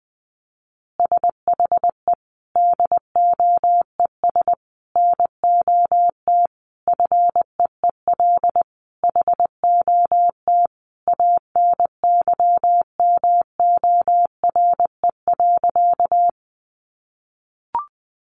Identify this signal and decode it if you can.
Morse code, heard as a keyed tone, about 20 wpm, 700 Hz. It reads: SHE DOES NOT FEEL HOT ANYMORE.
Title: Easy Phrases 20wpm